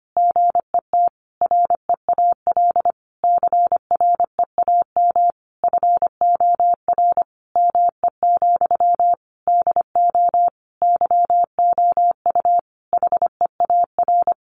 Morse code, heard as a keyed tone, about 25 wpm, 700 Hz. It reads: GET REAL CREAM FOR ME, DO YOU HEAR
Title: Sentences from Top 1500 Words 25wpm